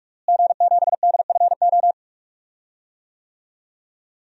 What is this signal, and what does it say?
Morse code, heard as a keyed tone, about 43 wpm, 700 Hz. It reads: G7DFO
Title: DXCC-100 Call Signs - 5 Characters - Repeated 2x - 45wpm